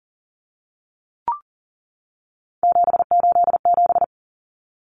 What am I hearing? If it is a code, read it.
Morse code, heard as a keyed tone, about 40 wpm, 700 Hz. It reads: 787
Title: Common Cut Numbers - 3-Digit 40wpm